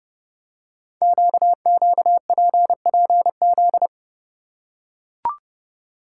Morse code, heard as a keyed tone, about 30 wpm, 700 Hz. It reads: QQPPZ
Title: Code Groups - 5 Letters - 30wpm